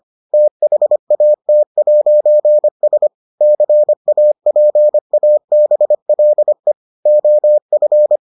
Morse code, heard as a keyed tone, about 25 wpm, 600 Hz. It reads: THAT'S CAPABLE OF